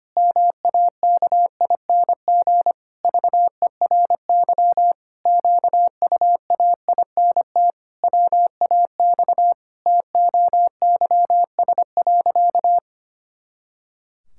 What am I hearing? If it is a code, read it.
Morse code, heard as a keyed tone, about 25 wpm, 700 Hz. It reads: MAKING VERY QUAINT WAX TOYS.
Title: Pangram Challenge v2 - 25wpm